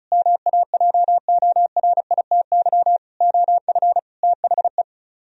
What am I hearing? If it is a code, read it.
Morse code, heard as a keyed tone, about 43 wpm, 700 Hz. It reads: MAJORITY OF THE
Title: Speed-Racing - Sets of 4 Words 35wpm